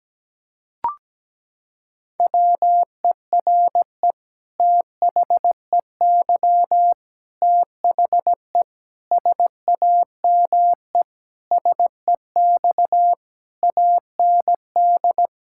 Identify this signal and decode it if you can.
Morse code, heard as a keyed tone, about 17 wpm, 700 Hz. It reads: WERE THEY THE SAME SEX AND
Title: Sentences from Top 1600 Words 17wpm